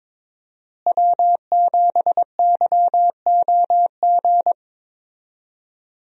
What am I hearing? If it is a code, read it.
Morse code, heard as a keyed tone, about 22 wpm, 690 Hz. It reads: W7YOG
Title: USA Call Signs - 5 Characters - Repeated 3x - 22wpm